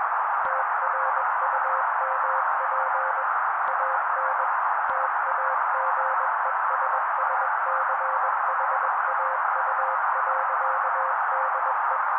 Morse code, heard as a keyed tone, about 21 wpm, 530 Hz. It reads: TRUMP AN TAGESSCHAU.DE